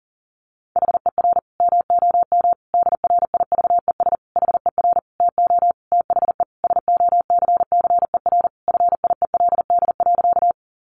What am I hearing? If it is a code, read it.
Morse code, heard as a keyed tone, about 37 wpm, 700 Hz. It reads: HER MOM DRIVES HER TO THE SOCCER FIELD.
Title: Easy Phrases 40wpm